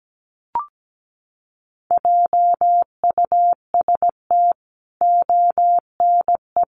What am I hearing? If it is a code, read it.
Morse code, heard as a keyed tone, about 17 wpm, 700 Hz. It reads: JUST ONE
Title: Sentences from Top 1000 Words 17wpm